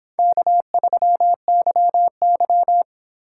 Morse code, heard as a keyed tone, about 25 wpm, 700 Hz. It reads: K3YY